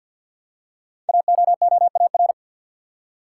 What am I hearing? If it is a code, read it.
Morse code, heard as a keyed tone, about 51 wpm, 700 Hz. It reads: AOOAR
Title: Code Groups - 5 Letter:Numbers - 50wpm